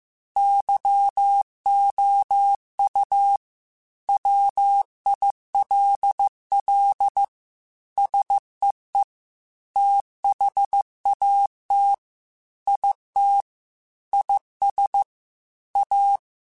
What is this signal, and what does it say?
Morse code, heard as a keyed tone, about 15 wpm, 800 Hz. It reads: YOU WILL SEE THAT IT IS A
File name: Technical Subjects / Technical Subjects-mp3s / OR Gate @13